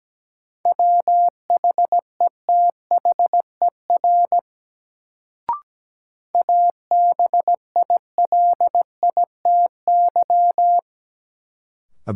Morse code, heard as a keyed tone, about 17 wpm, 700 Hz. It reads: WHETHER ABILITY
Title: Sets of 3 Words - New Word Drill - 17wpm